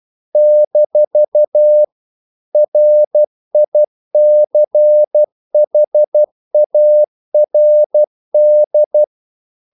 Morse code, heard as a keyed tone, about 12 wpm, 600 Hz. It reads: - RICHARD
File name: CW / Practice / EnglishQuotations0112